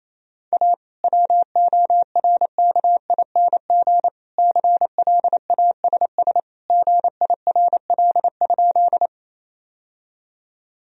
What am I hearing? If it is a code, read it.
Morse code, heard as a keyed tone, about 28 wpm, 700 Hz. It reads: A WORKING CLASS GIRL?
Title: Sentences from Top 1000 Words - Review Part 4 - 28wpm